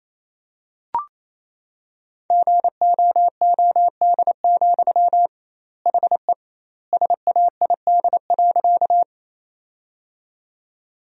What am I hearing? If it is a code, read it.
Morse code, heard as a keyed tone, about 28 wpm, 700 Hz. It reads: GOOD, HE SAID.
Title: 2x Word-Spacing - Sentences from Top 200 Words - 28wpm